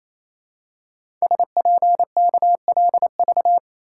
Morse code, heard as a keyed tone, about 27 wpm, 700 Hz. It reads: SPKLV
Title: Code Groups - 5 Letters - 28wpm